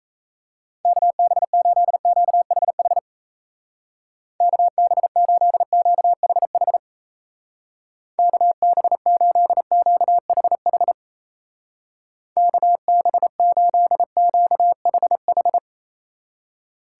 Morse code, heard as a keyed tone, about 33 wpm, 700 Hz. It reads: KB8QHH KB8QHH KB8QHH KB8QHH
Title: Speed Racing - USA 2x3 Call Signs 28wpm